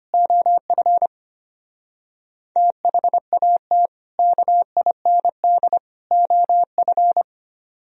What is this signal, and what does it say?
Morse code, heard as a keyed tone, about 32 wpm, 700 Hz. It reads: OF THAT KIND OF